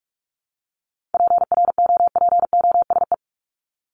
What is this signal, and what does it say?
Morse code, heard as a keyed tone, about 44 wpm, 700 Hz. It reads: PROPOSE
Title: ED Suffix - Top 100 Words with Base Words 45wpm